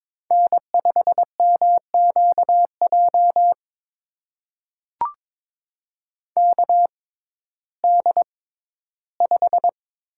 Morse code, heard as a keyed tone, about 22 wpm, spaced as for 14 wpm, 700 Hz. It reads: N5MQJ KD5
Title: USA Call Signs - 6x Character Spacing - 22wpm